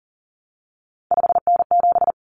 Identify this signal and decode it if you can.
Morse code, heard as a keyed tone, about 37 wpm, 700 Hz. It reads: 5N7
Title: Common Cut Numbers - 3-Digit 40wpm